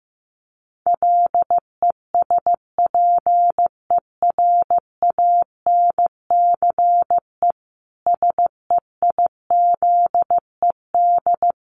Morse code, heard as a keyed tone, about 15 wpm, 700 Hz. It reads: LESPERANCE SEIZED
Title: A Sound Of Thunder 15wpm